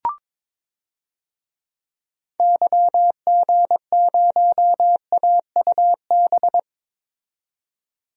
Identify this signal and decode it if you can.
Morse code, heard as a keyed tone, about 22 wpm, 700 Hz. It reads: YG0AUB
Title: DXCC-100 Call Signs - 6 Characters - 22wpm